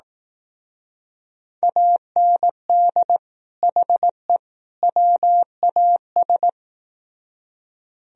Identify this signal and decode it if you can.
Morse code, heard as a keyed tone, about 19 wpm, 700 Hz. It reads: AND HE WAS
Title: Speed-Racing - Sets of 3 Words 15wpm